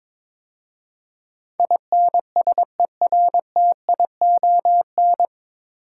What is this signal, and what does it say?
Morse code, heard as a keyed tone, about 22 wpm, 700 Hz. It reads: INSERTION
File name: 9 Letter Words - First 3 Letters Given - 22wpm